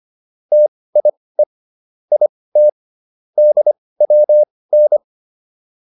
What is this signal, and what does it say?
Morse code, heard as a keyed tone, about 25 wpm, spaced as for 18 wpm, 600 Hz. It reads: TIE IT DWN